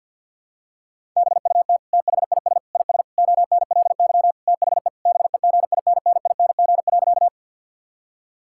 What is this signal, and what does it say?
Morse code, heard as a keyed tone, about 51 wpm, 690 Hz. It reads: BUT THIS IS ONLY THE BEGINNING.
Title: Sentences from Top 1000 Words - Review Part 5 - 50wpm